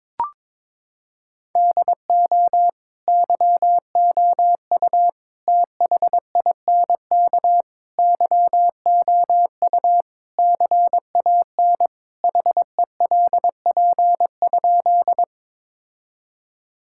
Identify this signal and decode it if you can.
Morse code, heard as a keyed tone, about 22 wpm, 700 Hz. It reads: DO YOU THINK YOU CAN HELP?
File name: practice / 22wpm / Sentences with 4-Word Phrases - 22wpm